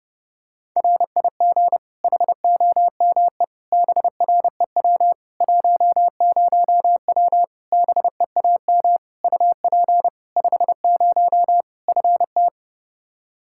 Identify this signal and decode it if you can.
Morse code, heard as a keyed tone, about 32 wpm, 700 Hz. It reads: RIG HOME BREW 10W BEAM UP 50 FT
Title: Practice QSO - Exchange 2 - Info - 30wpm